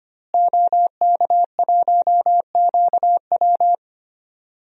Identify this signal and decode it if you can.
Morse code, heard as a keyed tone, about 25 wpm, 700 Hz. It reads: OK1QW
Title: DXCC-100 Call Signs - 5 Characters - Repeated 2x - 25wpm